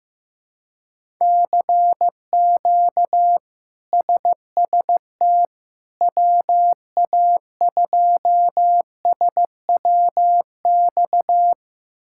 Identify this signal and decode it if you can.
Morse code, heard as a keyed tone, about 15 wpm, 700 Hz. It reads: CQ SST WA2SWX
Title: K1USN SST QSO Practice - 15wpm